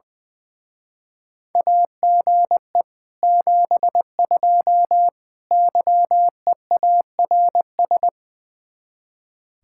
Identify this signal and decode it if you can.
Morse code, heard as a keyed tone, about 21 wpm, 700 Hz. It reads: AGE 72 YEARS
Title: Speed-Racing Common QSO Phrases - 20wpm